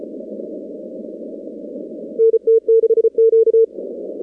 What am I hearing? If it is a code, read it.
Morse code, heard as a keyed tone, about 34 wpm, 450 Hz. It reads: NT6Q
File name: contests / 2019-cw-wpx-kt5j-k5pi- -2330 0-45